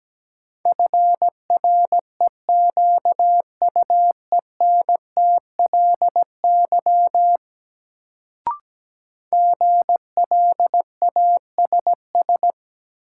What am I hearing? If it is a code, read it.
Morse code, heard as a keyed tone, about 17 wpm, 700 Hz. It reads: FREQUENTLY GLASS
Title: Sentences from Top 1300 Words - New Word Drill Alt - 17wpm